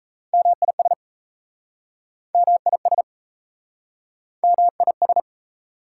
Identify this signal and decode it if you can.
Morse code, heard as a keyed tone, about 38 wpm, 700 Hz. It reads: MIS MIS MIS